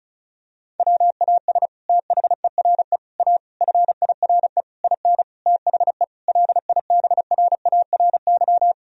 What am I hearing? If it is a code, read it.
Morse code, heard as a keyed tone, about 34 wpm, 700 Hz. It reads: WAS THERE A FIRE IN THE LIBRARY
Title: Sentences from Top 1000 Words - Review Part 4 - 35wpm